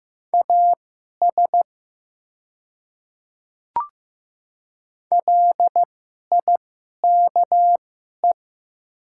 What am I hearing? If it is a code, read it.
Morse code, heard as a keyed tone, about 15 wpm, spaced as for 11 wpm, 700 Hz. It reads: AS LIKE
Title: 2x Character Spacing - Top 100 Words - 15wpm